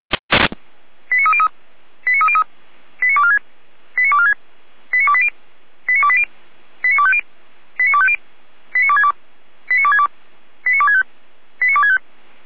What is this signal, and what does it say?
Five-tone selective call (ZVEI), heard as quick runs of five tones, tones of about 70 ms. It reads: 89303 89303 89246 89246 89280 89280 89250 89250 89262 89262 89265 89265
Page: ZVEI